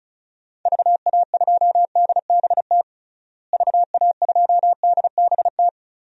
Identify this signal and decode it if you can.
Morse code, heard as a keyed tone, about 34 wpm, 700 Hz. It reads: VA2DBT VA2DBT
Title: DXCC-100 Call Signs - 6 Characters - Repeated 2x - 35wpm